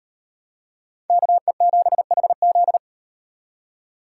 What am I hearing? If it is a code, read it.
Morse code, heard as a keyed tone, about 38 wpm, 700 Hz. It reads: KE7HZ